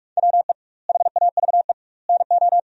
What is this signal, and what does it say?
Morse code, heard as a keyed tone, about 46 wpm, 700 Hz. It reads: WE HAVE NO
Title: 2x Word-Spacing - Sentences from Top 100 Words - 45wpm